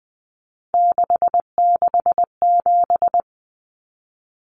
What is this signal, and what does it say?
Morse code, heard as a keyed tone, about 20 wpm, 700 Hz. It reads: 667
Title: Common Cut Numbers - 3-Digit 20wpm